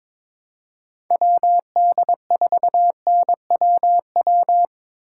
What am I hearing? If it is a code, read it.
Morse code, heard as a keyed tone, about 22 wpm, 700 Hz. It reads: WD4NWW
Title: USA Call Signs - 6 Characters - 4x Character Spacing - 22wpm